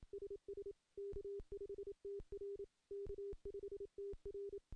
Morse code, heard as a keyed tone, about 27 wpm, 400 Hz. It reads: SS K5TR K5TR
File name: contests / CW-SS-k5tr- -0800